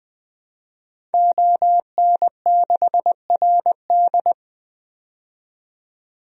Morse code, heard as a keyed tone, about 20 wpm, 700 Hz. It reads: ON6RD
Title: DXCC-100 Call Signs - 5 Characters - 20wpm